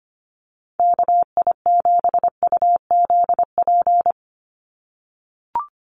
Morse code, heard as a keyed tone, about 25 wpm, 700 Hz. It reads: KI7UZP
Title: USA Call Signs 25wpm